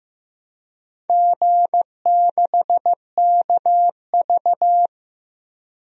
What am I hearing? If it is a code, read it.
Morse code, heard as a keyed tone, about 15 wpm, 700 Hz. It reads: G6KV
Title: DXCC-100 Call Signs - 4 Characters - 15wpm